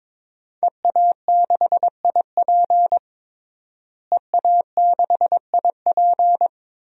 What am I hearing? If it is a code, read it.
Morse code, heard as a keyed tone, about 21 wpm, 700 Hz. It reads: EA6IP EA6IP
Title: DXCC-100 Call Signs - 5 Characters - Repeated 3x - 22wpm